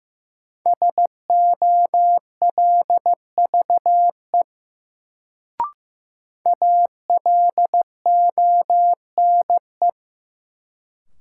Morse code, heard as a keyed tone, about 15 wpm, 700 Hz. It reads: SOLVE ALONE
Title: Sentences from Top 1000 Words - New Word Drill - 15wpm